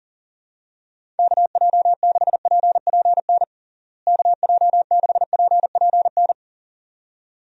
Morse code, heard as a keyed tone, about 40 wpm, 700 Hz. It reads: KJ6PPN KJ6PPN
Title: USA Call Signs - 6 Characters - Repeated 2x - 40wpm